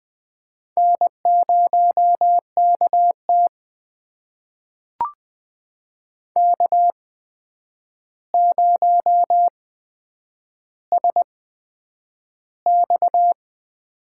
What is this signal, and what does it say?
Morse code, heard as a keyed tone, about 20 wpm, spaced as for 9 wpm, 700 Hz. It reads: N0KT K0SX
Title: USA Call Signs - 8x Character Spacing - 20wpm